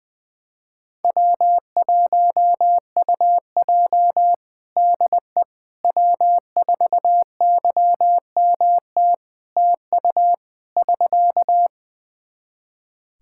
Morse code, heard as a keyed tone, about 20 wpm, 700 Hz. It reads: W1UJ DE W4YMT TU <SK>
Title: Practice QSO - Exchange 4 - Ending - 20wpm